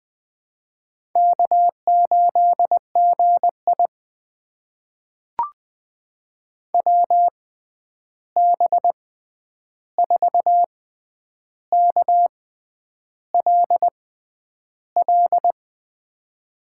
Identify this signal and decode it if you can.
Morse code, heard as a keyed tone, about 21 wpm, spaced as for 9 wpm, 700 Hz. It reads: K8GI WB4KLL
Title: USA Call Signs - 6x Character Spacing - 20wpm